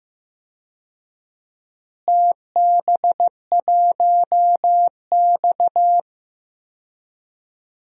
Morse code, heard as a keyed tone, about 15 wpm, 700 Hz. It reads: TB1X
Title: DXCC-100 Call Signs - 4 Characters - Repeated 2x - 15wpm